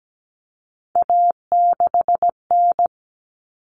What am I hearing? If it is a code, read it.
Morse code, heard as a keyed tone, about 17 wpm, 700 Hz. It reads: A6N
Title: All Cut Numbers - 3-Digit 17wpm